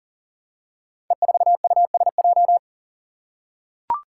Morse code, heard as a keyed tone, about 41 wpm, 700 Hz. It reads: E4USJ
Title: Five Challenging Letter-Number Combinations 40wpm